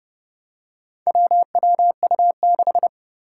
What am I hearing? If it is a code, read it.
Morse code, heard as a keyed tone, about 30 wpm, 700 Hz. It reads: WWU6
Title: Four Challenging Letter-Number Combinations 30wpm